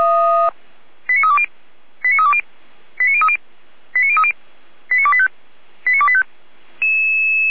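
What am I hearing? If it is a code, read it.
Five-tone selective call (ZVEI), heard as quick runs of five tones, tones of about 80 ms. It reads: 89320 89320 89030 89030 89285 89285
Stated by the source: ZVEI five-tone selective call 89305 received on SDR, served by feuerwehr-orlamuende (f85360).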